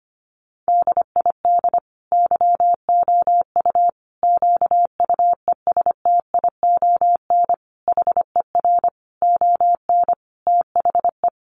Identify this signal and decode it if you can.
Morse code, heard as a keyed tone, about 25 wpm, 700 Hz. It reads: DID YOU QUESTION HER ON THE
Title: Sentences from Top 800 Words 25wpm